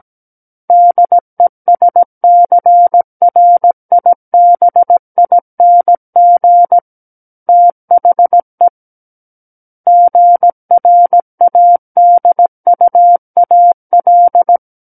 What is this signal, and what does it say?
Morse code, heard as a keyed tone, about 17 wpm, 700 Hz. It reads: DESCRIBING THE GRADUAL
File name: Book One / 08-Chapter 8 - 17 WPM